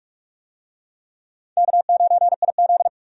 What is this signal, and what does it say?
Morse code, heard as a keyed tone, about 46 wpm, 700 Hz. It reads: K9IZ
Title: USA Call Signs - 4 Characters - Repeated 2x - 45wpm